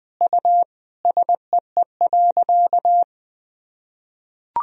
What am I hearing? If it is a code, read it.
Morse code, heard as a keyed tone, about 20 wpm, 700 Hz. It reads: U SEE.
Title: Sentences from Top 100 Words and Abbreviations 20wpm